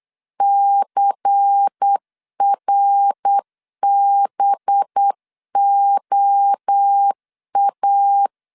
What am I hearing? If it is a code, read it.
Morse code, heard as a keyed tone, about 8 wpm, 800 Hz. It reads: CRBOA
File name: LEZIONI / 6 PKX0 / 6 PK 8wpm 3003